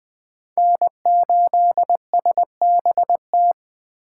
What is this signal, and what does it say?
Morse code, heard as a keyed tone, about 20 wpm, 700 Hz. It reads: N8SBT
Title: Speed Racing - USA Call Signs 17wpm